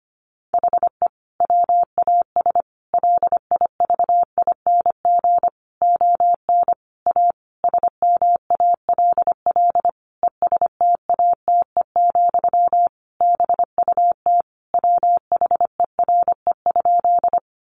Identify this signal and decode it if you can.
Morse code, heard as a keyed tone, about 25 wpm, 700 Hz. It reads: HE WAS LIVING ON A SMALL ESTATE, BUT WHERE?
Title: Sentences from Top 1200 Words 25wpm